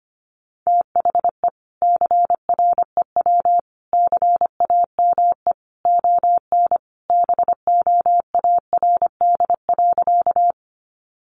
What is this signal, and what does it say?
Morse code, heard as a keyed tone, about 25 wpm, 700 Hz. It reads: THE CREW CAME ON BOARD.
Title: Sentences from Top 1500 Words 25wpm